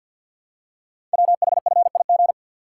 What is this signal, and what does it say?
Morse code, heard as a keyed tone, about 48 wpm, 700 Hz. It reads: WHFIG
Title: Code Groups - 5 Letters - 50wpm